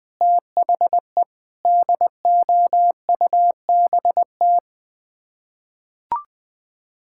Morse code, heard as a keyed tone, about 20 wpm, 700 Hz. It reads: THE DOUBT
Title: Sets of 8 Words - 20wpm